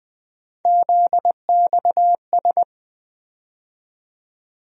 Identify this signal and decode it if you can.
Morse code, heard as a keyed tone, about 20 wpm, 700 Hz. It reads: ZXS
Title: USA Call Sign Postfixes - 3 Letters 20wpm